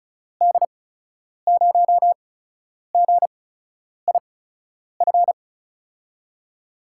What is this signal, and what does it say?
Morse code, heard as a keyed tone, about 36 wpm, spaced as for 10 wpm, 700 Hz. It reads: D0GIF
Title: USA Call Signs - 6 Characters - 8x Character Spacing - 35wpm